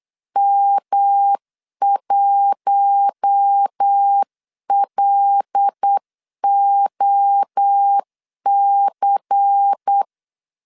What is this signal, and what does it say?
Morse code, heard as a keyed tone, about 8 wpm, 800 Hz. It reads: M1LOC
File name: CW / DET / 1 6 1500C S C 8wpm 5219